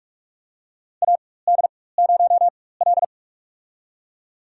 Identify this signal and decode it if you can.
Morse code, heard as a keyed tone, about 46 wpm, spaced as for 21 wpm, 700 Hz. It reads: AD0L
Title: USA Call Signs - 4 Characters - 4x Character Spacing - 45wpm